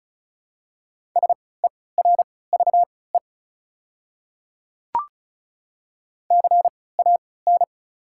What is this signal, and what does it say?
Morse code, heard as a keyed tone, about 36 wpm, spaced as for 20 wpm, 690 Hz. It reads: SERVE CAN
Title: 3x Character Spacing - Top 500 Words - 35wpm